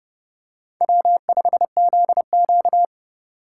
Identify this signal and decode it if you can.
Morse code, heard as a keyed tone, about 30 wpm, 700 Hz. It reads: W5ZQ
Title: USA Call Signs - 4 Characters - 4x Character Spacing - 30wpm